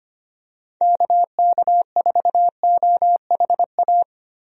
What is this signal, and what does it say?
Morse code, heard as a keyed tone, about 25 wpm, 700 Hz. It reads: KK4OHA
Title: USA Call Signs - 6 Characters - Repeated 3x - 25wpm